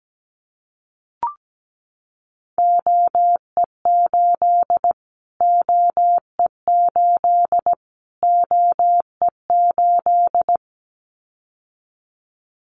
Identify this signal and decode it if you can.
Morse code, heard as a keyed tone, about 17 wpm, 700 Hz. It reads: OE8 OE8 OE8
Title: Callsign Prefixes for Top 25 DXCC - Repeated 3x - 17wpm